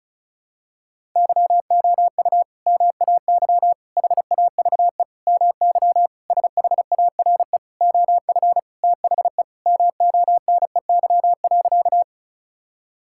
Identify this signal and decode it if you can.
Morse code, heard as a keyed tone, about 36 wpm, 700 Hz. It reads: YOU MAY HAVE MY SHARE OF THE MONEY.
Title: Sentences from Top 1000 Words - Review Part 6 - 35wpm